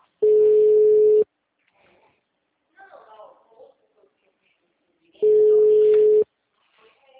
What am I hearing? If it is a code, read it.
Ringback tone (Europe, 425 Hz), with speech.
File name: out- -2013- - - .49655